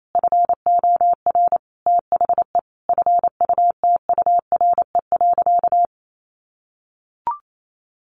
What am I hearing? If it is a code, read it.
Morse code, heard as a keyed tone, about 28 wpm, 700 Hz. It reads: FOR THE FUTURE.
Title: Sentences from Top 1400 Words 28wpm